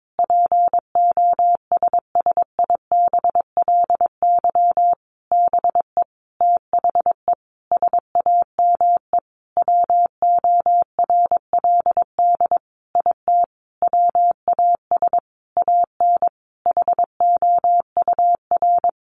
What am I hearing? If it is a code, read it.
Morse code, heard as a keyed tone, about 22 wpm, 700 Hz. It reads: POSSIBLY BE THE SAME WORLD IT WAS AN HOUR
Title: Sentences from Top 1300 Words 22wpm